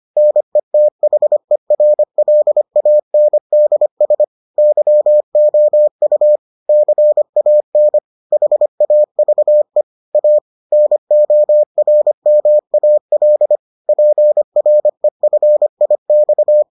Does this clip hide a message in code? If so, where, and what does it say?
Yes: Morse code from 0.2 to 16.6 s, heard as a keyed tone, about 25 wpm, 600 Hz. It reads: NETHERLANDS YOU CAN HAVE A NORMAL PREFIX